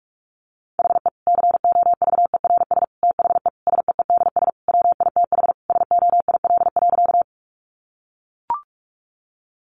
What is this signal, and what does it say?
Morse code, heard as a keyed tone, about 44 wpm, 700 Hz. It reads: HE COVERS THE SEEDS WITH SOIL.
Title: Easy Phrases 45wpm